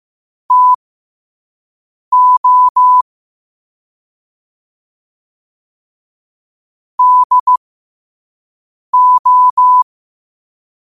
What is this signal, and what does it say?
Morse code, heard as a keyed tone, about 15 wpm, spaced as for 5 wpm, 1000 Hz. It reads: TO DO
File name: qotdfiles / 5wpm